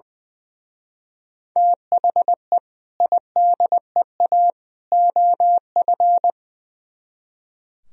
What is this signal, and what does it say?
Morse code, heard as a keyed tone, about 20 wpm, 700 Hz. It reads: THE IDEA OF
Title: Speed-Racing - Sets of 3 Words 20wpm